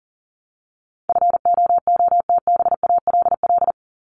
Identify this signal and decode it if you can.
Morse code, heard as a keyed tone, about 40 wpm, 700 Hz. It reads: FOOTBALL
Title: Top 1501-1750 Words 40wpm